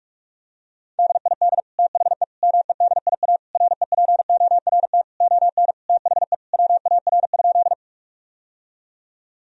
Morse code, heard as a keyed tone, about 42 wpm, 700 Hz. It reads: DID THE MEDIA REPORT ON THE WAR?
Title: Sentences from Top 1000 Words - Review Part 4 - 45wpm